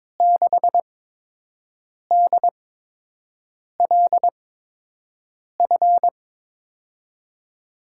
Morse code, heard as a keyed tone, about 22 wpm, spaced as for 6 wpm, 700 Hz. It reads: 6DLF
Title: USA Call Signs - 6 Characters - 8x Character Spacing - 22wpm